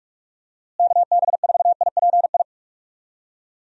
Morse code, heard as a keyed tone, about 46 wpm, 700 Hz. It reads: KB4IPI
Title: USA Call Signs - 6 Characters - Repeated 2x - 45wpm